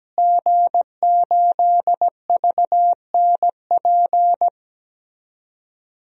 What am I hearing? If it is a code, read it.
Morse code, heard as a keyed tone, about 17 wpm, 700 Hz. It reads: G8VNP
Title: DXCC-100 Call Signs - 5 Characters - Repeated 3x - 17wpm